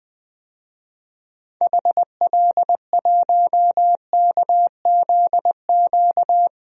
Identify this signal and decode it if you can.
Morse code, heard as a keyed tone, about 20 wpm, 700 Hz. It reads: HL1KZQ
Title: DXCC-100 Call Signs - 6 Characters - Repeated 2x - 20wpm